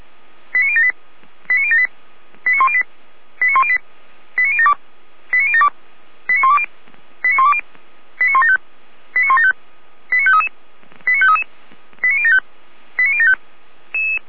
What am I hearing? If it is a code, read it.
Five-tone selective call (ZVEI), heard as quick runs of five tones, tones of about 70 ms. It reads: 89078 89078 89108 89108 89072 89072 89120 89120 89175 89175 89533 89533 89075 89075
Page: audio.f85360.feuerwehr-orlamuende.org